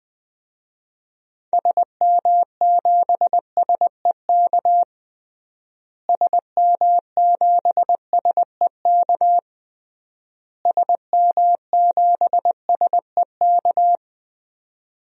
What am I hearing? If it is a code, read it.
Morse code, heard as a keyed tone, about 20 wpm, 700 Hz. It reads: SM7SEK SM7SEK SM7SEK
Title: DXCC-100 Call Signs - 6 Characters - Repeated 3x - 20wpm